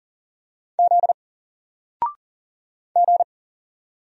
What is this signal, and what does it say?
Morse code, heard as a keyed tone, about 39 wpm, 700 Hz. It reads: Z G
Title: Single Character (Letter:Number:Punctuation:Prosign) 40wpm